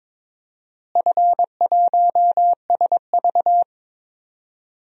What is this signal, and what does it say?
Morse code, heard as a keyed tone, about 22 wpm, 700 Hz. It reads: F1SV
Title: DXCC-100 Call Signs - 4 Characters - Repeated 2x - 22wpm